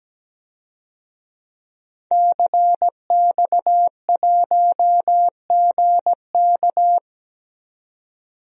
Morse code, heard as a keyed tone, about 17 wpm, 700 Hz. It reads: CX1GK
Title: DXCC-100 Call Signs - 5 Characters - 17wpm